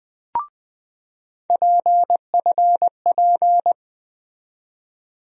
Morse code, heard as a keyed tone, about 20 wpm, 700 Hz. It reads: PFP
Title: USA Call Sign Postfixes - 3 Letters 20wpm